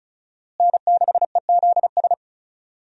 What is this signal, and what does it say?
Morse code, heard as a keyed tone, about 36 wpm, 700 Hz. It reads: N6EZS
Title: USA Call Signs - 5 Characters - Repeated 2x - 35wpm